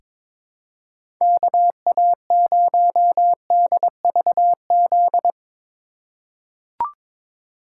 Morse code, heard as a keyed tone, about 22 wpm, 700 Hz. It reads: KA0DVZ
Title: USA Call Signs - 6 Characters - 3x Character Spacing - 22wpm